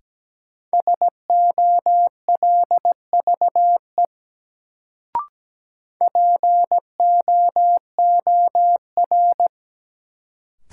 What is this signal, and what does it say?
Morse code, heard as a keyed tone, about 17 wpm, 700 Hz. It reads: SOLVE POOR
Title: Sentences from Top 1000 Words - New Word Drill - 17wpm